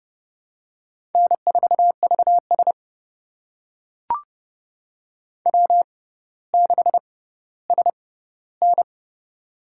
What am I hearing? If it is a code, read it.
Morse code, heard as a keyed tone, about 30 wpm, spaced as for 15 wpm, 700 Hz. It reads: N4VS W6SN
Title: USA Call Signs - 4 Characters - 6x Character Spacing - 30wpm